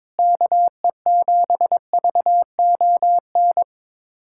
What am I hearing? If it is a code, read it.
Morse code, heard as a keyed tone, about 22 wpm, 700 Hz. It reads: KE7VON
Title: USA Callsign DE Callsign - 6 Characters - 22wpm